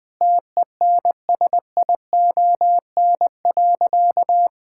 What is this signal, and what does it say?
Morse code, heard as a keyed tone, about 21 wpm, 700 Hz. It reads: TENSION.
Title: Sentences from Top 1600 Words 20wpm